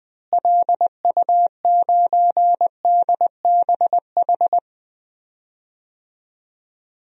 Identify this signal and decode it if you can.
Morse code, heard as a keyed tone, about 20 wpm, 700 Hz. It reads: LU9DBH
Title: DXCC-100 Call Signs - 6 Characters - Repeated 2x - 20wpm